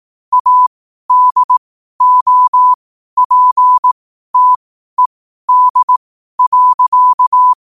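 Morse code, heard as a keyed tone, about 18 wpm, spaced as for 12 wpm, 1000 Hz. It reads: ADOPTED.
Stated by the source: Morse code at 12 wpm.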